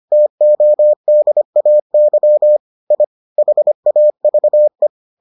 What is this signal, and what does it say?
Morse code, heard as a keyed tone, about 25 wpm, 600 Hz. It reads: TODAY I HAVE